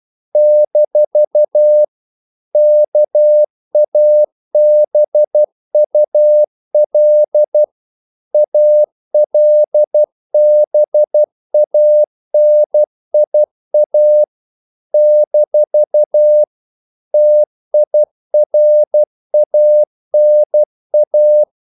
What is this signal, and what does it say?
Morse code, heard as a keyed tone, about 12 wpm, 600 Hz. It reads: - KABUL ALBANIA - TIRANA
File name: CW / Practice / NationalCapitals12